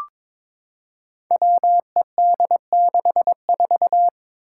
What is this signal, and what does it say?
Morse code, heard as a keyed tone, about 22 wpm, 700 Hz. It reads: WED64
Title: Five Challenging Letter-Number Combinations 22wpm